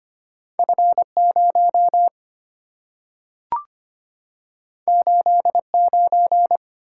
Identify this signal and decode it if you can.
Morse code, heard as a keyed tone, about 25 wpm, 700 Hz. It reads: F0 89
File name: Two Letter-Number Combinations - 25wpm